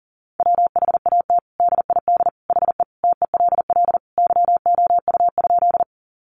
Morse code, heard as a keyed tone, about 40 wpm, 700 Hz. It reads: WHAT DID HE TELL YOU?
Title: Sentences from Top 200 Words 40wpm